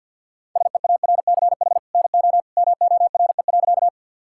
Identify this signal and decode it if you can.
Morse code, heard as a keyed tone, about 50 wpm, 700 Hz. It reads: SEARCH NO MORE.
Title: Sentences from Top 1000 Words - Review Part 5 - 50wpm